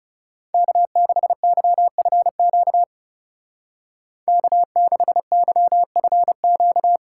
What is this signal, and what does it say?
Morse code, heard as a keyed tone, about 37 wpm, 700 Hz. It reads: K6YFQ K6YFQ
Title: Speed Racing - USA 1x3 Call Signs 30wpm